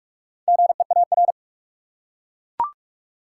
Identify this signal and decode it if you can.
Morse code, heard as a keyed tone, about 42 wpm, 700 Hz. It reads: GEAR
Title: Four Letter Words - 45wpm